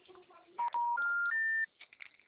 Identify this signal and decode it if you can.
Special information tone.
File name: out- -2032- - - .64259